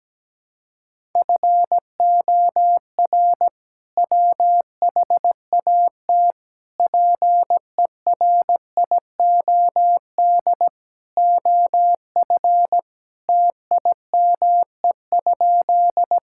Morse code, heard as a keyed tone, about 17 wpm, 700 Hz. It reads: FOR WHAT PERIOD OF TIME?
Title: Sentences from Top 1000 Words - Review Part 5 - 17wpm